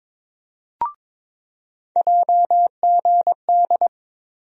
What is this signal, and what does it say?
Morse code, heard as a keyed tone, about 22 wpm, 700 Hz. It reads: JGD
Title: USA Call Sign Postfixes - 3 Letters 22wpm